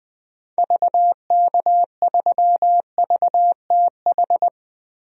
Morse code, heard as a keyed tone, about 20 wpm, 700 Hz. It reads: VK3VTH
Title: DXCC-100 Call Signs - 6 Characters - 20wpm